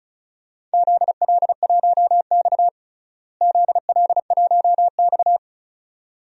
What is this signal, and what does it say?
Morse code, heard as a keyed tone, about 34 wpm, 700 Hz. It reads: ZL1X ZL1X
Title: DXCC-100 Call Signs - 4 Characters - Repeated 2x - 35wpm